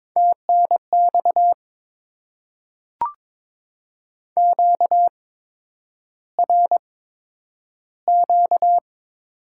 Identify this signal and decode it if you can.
Morse code, heard as a keyed tone, about 22 wpm, spaced as for 9 wpm, 700 Hz. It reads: TNX QRQ
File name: Top 90 QSO Elements - 8x Character Spacing - 22wpm